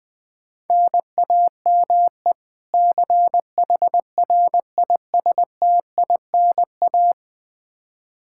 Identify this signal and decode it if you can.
Morse code, heard as a keyed tone, about 20 wpm, 700 Hz. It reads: NAME CHRISTINA
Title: Practice QSO - Exchange 1 - Beginning - 20wpm